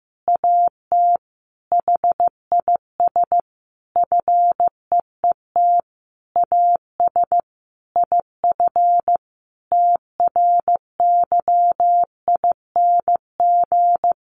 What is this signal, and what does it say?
Morse code, heard as a keyed tone, about 15 wpm, 700 Hz. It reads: AT HIS FEET AS IF TRYING
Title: A Sound Of Thunder 15wpm